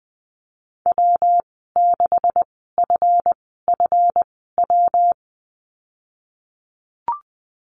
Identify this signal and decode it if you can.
Morse code, heard as a keyed tone, about 20 wpm, spaced as for 15 wpm, 700 Hz. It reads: W6FFW
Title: 161 Lesson 26g - Letter g - Call Signs 20wpm